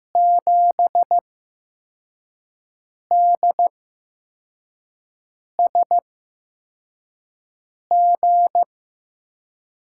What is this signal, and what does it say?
Morse code, heard as a keyed tone, about 15 wpm, spaced as for 4 wpm, 700 Hz. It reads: 7DSG
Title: USA Call Signs - 5 Characters - 8x Character Spacing - 15wpm